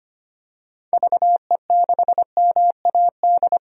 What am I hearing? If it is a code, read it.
Morse code, heard as a keyed tone, about 25 wpm, 700 Hz. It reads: VE6MAD
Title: DXCC-100 Call Signs - 6 Characters - 25wpm